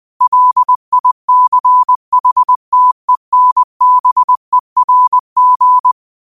Morse code, heard as a keyed tone, about 20 wpm, 1000 Hz. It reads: LICHTENBERG
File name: qotdfiles / 20wpm